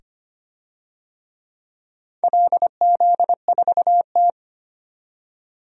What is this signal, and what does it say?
Morse code, heard as a keyed tone, about 25 wpm, 700 Hz. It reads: LZ4T
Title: DXCC-100 Call Signs - 4 Characters - 25wpm